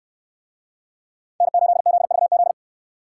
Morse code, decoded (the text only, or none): N8ZVZ